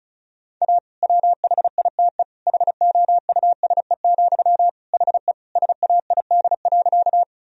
Morse code, heard as a keyed tone, about 34 wpm, 700 Hz. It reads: A WHITE HOUSE, HE SAID.
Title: Sentences from Top 1000 Words - Review Part 1 - 35wpm